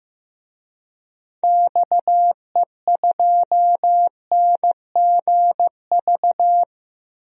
Morse code, heard as a keyed tone, about 15 wpm, 700 Hz. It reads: XE2NGV
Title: DXCC-100 Call Signs - 6 Characters - Repeated 3x - 15wpm